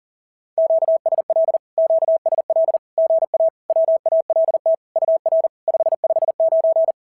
Morse code, heard as a keyed tone, about 40 wpm, 650 Hz. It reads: QSL QSL GA WALT UR 559
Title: SOTA QSO Practice 40wpm